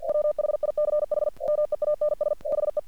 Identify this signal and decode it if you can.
Morse code, heard as a keyed tone, about 47 wpm, 600 Hz. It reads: WHICH MEANS HE
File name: archived / CRSnet